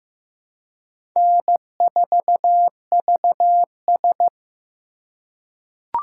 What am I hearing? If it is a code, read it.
Morse code, heard as a keyed tone, about 15 wpm, 700 Hz. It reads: N4VS
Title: USA Call Signs - 4 Characters - 4x Character Spacing - 15wpm